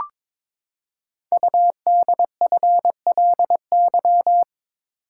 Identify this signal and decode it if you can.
Morse code, heard as a keyed tone, about 23 wpm, 700 Hz. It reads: UDFLY
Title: Code Groups - 5 Letters - 22wpm